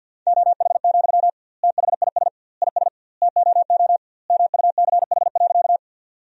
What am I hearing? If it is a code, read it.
Morse code, heard as a keyed tone, about 50 wpm, 700 Hz. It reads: OH, THIS IS TOO MUCH.
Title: 2x Word-Spacing - Sentences from Top 500 Words - 50wpm